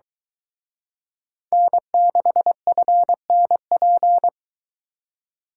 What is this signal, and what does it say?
Morse code, heard as a keyed tone, about 23 wpm, 700 Hz. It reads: N6FNP